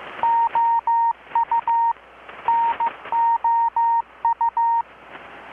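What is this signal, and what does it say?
Morse code, heard as a keyed tone, about 15 wpm, 930 Hz. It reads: OU NOU
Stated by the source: Morse from KPH's Night of Nights XX maritime radiotelegraphy event, received in Chiapas, Mexico about 3,800 km away on a Kenwood R-600 with a grounded 10 m long-wire antenna.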